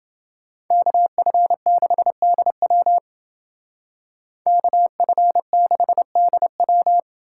Morse code, heard as a keyed tone, about 29 wpm, 700 Hz. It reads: KF6DW KF6DW